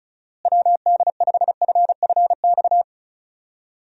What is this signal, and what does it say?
Morse code, heard as a keyed tone, about 36 wpm, 700 Hz. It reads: WD5FFX
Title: USA Call Signs - 6 Characters - Repeated 3x - 35wpm